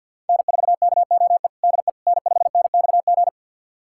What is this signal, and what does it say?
Morse code, heard as a keyed tone, about 45 wpm, 700 Hz. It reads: N4KOE DE N5NXG
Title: USA Callsign DE Callsign - 5 Characters - 50wpm